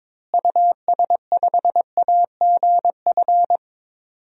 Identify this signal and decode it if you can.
Morse code, heard as a keyed tone, about 22 wpm, 700 Hz. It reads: US5AGF
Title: DXCC-100 Call Signs - 6 Characters - Repeated 2x - 22wpm